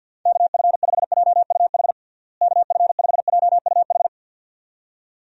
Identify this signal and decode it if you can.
Morse code, heard as a keyed tone, about 50 wpm, 700 Hz. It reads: KF5JUH KF5JUH
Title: USA Call Signs - 6 Characters - Repeated 2x - 50wpm